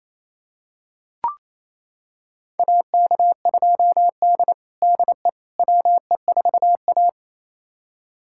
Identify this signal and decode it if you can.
Morse code, heard as a keyed tone, about 28 wpm, 700 Hz. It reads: AK2D DE WE4A
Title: USA Callsign DE Callsign - 4 Characters - 28wpm